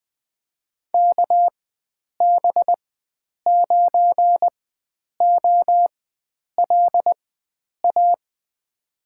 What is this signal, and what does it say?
Morse code, heard as a keyed tone, about 21 wpm, spaced as for 9 wpm, 700 Hz. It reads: KB9OLA
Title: USA Call Signs - 4x Character Spacing - 20wpm